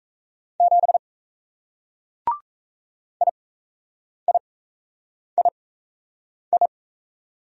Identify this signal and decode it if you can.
Morse code, heard as a keyed tone, about 38 wpm, spaced as for 8 wpm, 700 Hz. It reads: 7 IIII